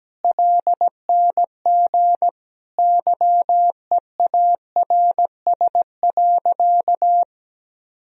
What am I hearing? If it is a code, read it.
Morse code, heard as a keyed tone, about 17 wpm, 700 Hz. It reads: LNG YEARS.
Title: Sentences from Top 200 Words and Abbreviations 17wpm